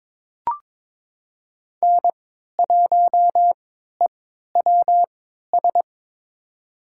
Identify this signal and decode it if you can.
Morse code, heard as a keyed tone, about 22 wpm, spaced as for 13 wpm, 700 Hz. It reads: N1EWS
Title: USA Call Signs - 5 Characters - 3x Character Spacing - 22wpm